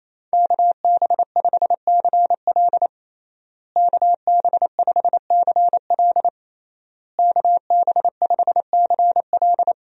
Morse code, heard as a keyed tone, about 27 wpm, 700 Hz. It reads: KB5CL KB5CL KB5CL
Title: USA Call Signs - 5 Characters - Repeated 3x - 28wpm